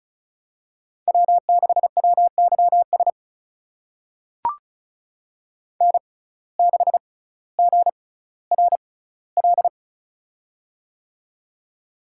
Morse code, heard as a keyed tone, about 36 wpm, spaced as for 18 wpm, 700 Hz. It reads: W6WYS N6GRL
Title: USA Call Signs - 5 Characters - 6x Character Spacing - 35wpm